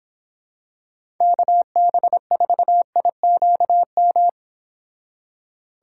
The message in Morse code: KB4IQM

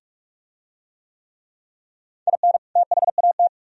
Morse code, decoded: IN THAT

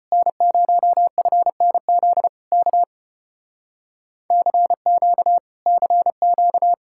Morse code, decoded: N0FNZ K CQ CQ

